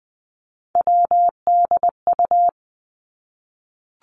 Morse code, heard as a keyed tone, about 20 wpm, 700 Hz. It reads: WDU
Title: Three Challenging Letter-Number Combinations 20wpm